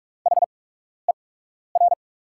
Morse code, heard as a keyed tone, about 43 wpm, spaced as for 12 wpm, 700 Hz. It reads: HER
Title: 8x Character Spacing - Top 100 Words - 45wpm